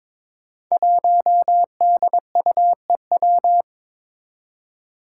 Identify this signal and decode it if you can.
Morse code, heard as a keyed tone, about 22 wpm, 700 Hz. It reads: 1DUEW
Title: Five Challenging Letter-Number Combinations 22wpm